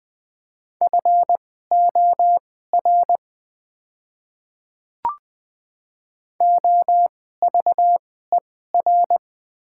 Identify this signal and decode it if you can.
Morse code, heard as a keyed tone, about 20 wpm, spaced as for 14 wpm, 690 Hz. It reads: FOR OVER
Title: 2x Character Spacing - Top 100 Words - 20wpm